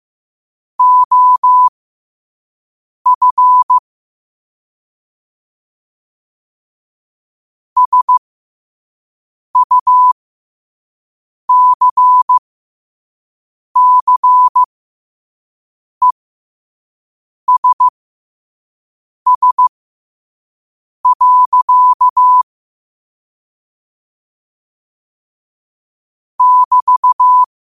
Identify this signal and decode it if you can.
Morse code, heard as a keyed tone, about 15 wpm, spaced as for 5 wpm, 1000 Hz. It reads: OF SUCCESS. <BT>